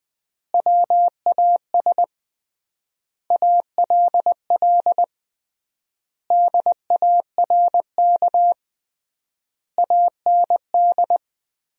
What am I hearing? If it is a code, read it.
Morse code, heard as a keyed tone, about 20 wpm, 700 Hz. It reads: WAS ALL DARK AND
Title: 3x Word-Spacing - Sentences from Top 500 Words - 20wpm